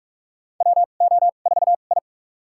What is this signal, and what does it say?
Morse code, heard as a keyed tone, about 45 wpm, spaced as for 32 wpm, 700 Hz. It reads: WO4I
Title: USA Call Signs - 4 Characters - 2x Character Spacing - 45wpm